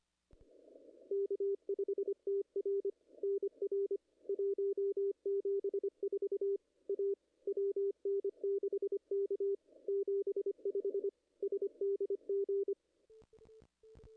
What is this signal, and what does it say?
Morse code, heard as a keyed tone, about 25 wpm, 410 Hz. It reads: K5TR NR 174 A WN6K 75 SDG
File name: contests / ss-cw-k5tr-op-n5rz -0100